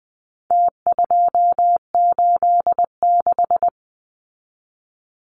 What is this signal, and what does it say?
Morse code, heard as a keyed tone, about 20 wpm, 700 Hz. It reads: T286
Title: Numbers - Contest Style - 4-Digit 20wpm